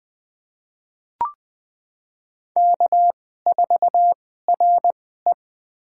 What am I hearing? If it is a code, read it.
Morse code, heard as a keyed tone, about 20 wpm, spaced as for 15 wpm, 700 Hz. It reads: K4RE
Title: USA Call Signs - 2x Character Spacing - 20wpm